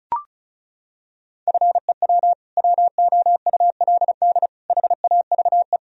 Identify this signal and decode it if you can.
Morse code, heard as a keyed tone, about 34 wpm, 700 Hz. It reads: FEW WOULD HAVE
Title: Sentences from Top 1000 Words - Review Part 1 - 35wpm